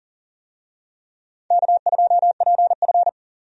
Morse code, heard as a keyed tone, about 38 wpm, 700 Hz. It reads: K2PF
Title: USA Call Signs - 4 Characters - 40wpm